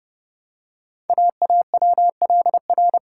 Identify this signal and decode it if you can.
Morse code, heard as a keyed tone, about 30 wpm, 700 Hz. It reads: AAWLR
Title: Code Groups - 5 Letters - 30wpm